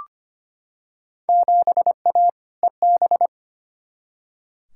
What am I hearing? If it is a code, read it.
Morse code, heard as a keyed tone, about 25 wpm, 700 Hz. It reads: 7A EB
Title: Field Day Exchanges Encore - 25wpm